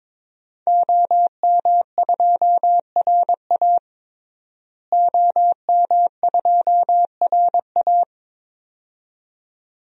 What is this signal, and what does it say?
Morse code, heard as a keyed tone, about 22 wpm, 700 Hz. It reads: OM2RA OM2RA
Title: DXCC-100 Call Signs - 5 Characters - Repeated 2x - 22wpm